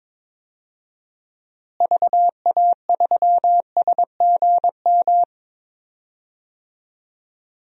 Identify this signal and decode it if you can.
Morse code, heard as a keyed tone, about 21 wpm, 700 Hz. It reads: VA3SGM
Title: DXCC-100 Call Signs - 6 Characters - Repeated 2x - 22wpm